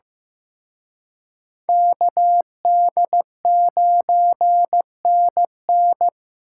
Morse code, heard as a keyed tone, about 18 wpm, 700 Hz. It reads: KD9NN
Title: Speed Racing - USA Call Signs 15wpm